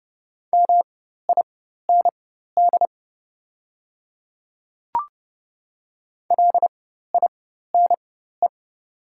Morse code, heard as a keyed tone, about 30 wpm, spaced as for 14 wpm, 700 Hz. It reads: MIND LINE
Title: 4x Character Spacing - Top 500 Words - 30wpm